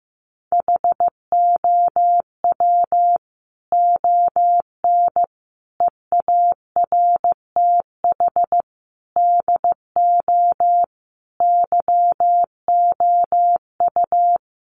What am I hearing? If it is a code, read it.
Morse code, heard as a keyed tone, about 15 wpm, 700 Hz. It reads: HOW ON EARTH DO YOU
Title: Sentences from Top 200 Words 15wpm